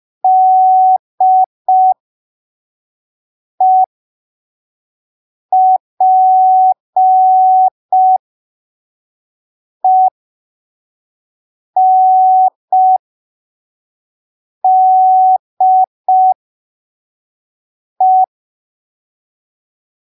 Morse code, heard as a keyed tone, about 5 wpm, spaced as for 3 wpm, 750 Hz. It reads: DEPENDE
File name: arquivos / texto / texto 22 05ppm